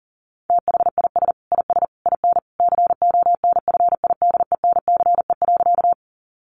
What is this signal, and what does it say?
Morse code, heard as a keyed tone, about 40 wpm, 700 Hz. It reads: THIS IS IN CONFIDENCE.
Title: Sentences from Top 1500 Words 40wpm